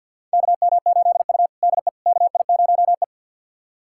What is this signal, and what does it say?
Morse code, heard as a keyed tone, about 47 wpm, 700 Hz. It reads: KM8U DE KI0E
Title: USA Callsign DE Callsign - 4 Characters - 50wpm